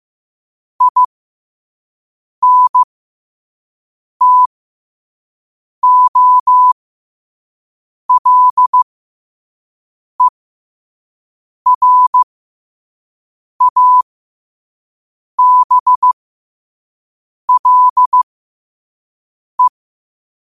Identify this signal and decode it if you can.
Morse code, heard as a keyed tone, about 15 wpm, spaced as for 5 wpm, 1000 Hz. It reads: INTOLERABLE
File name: qotdfiles / 5wpm